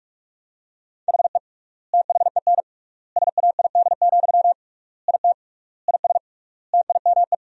Morse code, heard as a keyed tone, about 41 wpm, 700 Hz. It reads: HE THEN SAID, IT IS TIME
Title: 3x Word-Spacing - Sentences from Top 300 Words - 45wpm